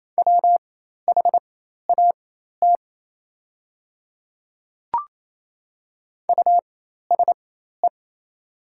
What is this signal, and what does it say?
Morse code, heard as a keyed tone, about 28 wpm, spaced as for 13 wpm, 690 Hz. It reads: WHAT USE
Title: 4x Character Spacing - Top 100 Words - 28wpm